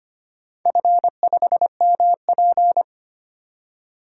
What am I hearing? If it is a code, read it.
Morse code, heard as a keyed tone, about 25 wpm, 700 Hz. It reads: F5MP
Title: DXCC-100 Call Signs - 4 Characters - 25wpm